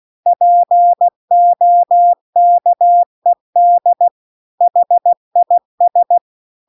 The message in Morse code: POKED HIS

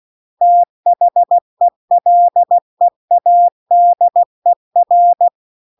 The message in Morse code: THE LEADER

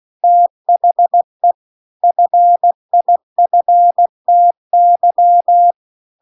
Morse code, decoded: THE FIFTY

